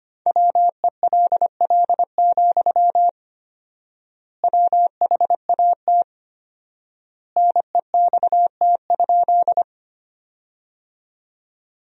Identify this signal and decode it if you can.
Morse code, heard as a keyed tone, about 25 wpm, 700 Hz. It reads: WELL, WHAT NEXT?
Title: 4x Word-Spacing - Sentences from Top 300 Words - 25wpm